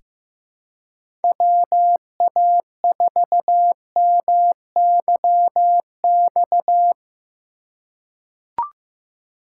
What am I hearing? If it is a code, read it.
Morse code, heard as a keyed tone, about 15 wpm, 700 Hz. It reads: WA4MYX
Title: USA Call Signs - 6 Characters - 3x Character Spacing - 15wpm